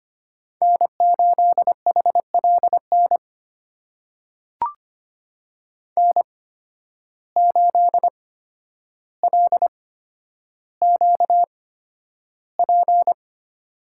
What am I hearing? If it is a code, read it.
Morse code, heard as a keyed tone, about 25 wpm, spaced as for 11 wpm, 700 Hz. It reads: N8HLN N8LQP
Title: USA Call Signs - 5 Characters - 8x Character Spacing - 25wpm